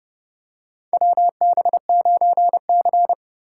Morse code, heard as a keyed tone, about 32 wpm, 700 Hz. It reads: WB9C
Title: USA Call Signs - 4 Characters - 3x Character Spacing - 30wpm